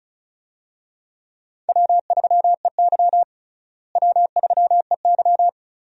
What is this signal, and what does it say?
Morse code, heard as a keyed tone, about 36 wpm, 700 Hz. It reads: W3EY W3EY
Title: USA Call Signs - 4 Characters - Repeated 2x - 35wpm